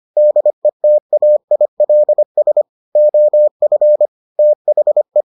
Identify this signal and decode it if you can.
Morse code, heard as a keyed tone, about 25 wpm, 600 Hz. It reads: DETAILS OF THE